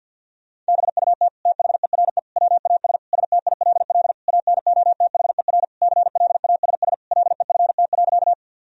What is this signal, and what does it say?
Morse code, heard as a keyed tone, about 50 wpm, 700 Hz. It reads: BUT THERE WAS STILL ANOTHER CLASS LEFT.
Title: Sentences from Top 1000 Words - Review Part 2 - 50wpm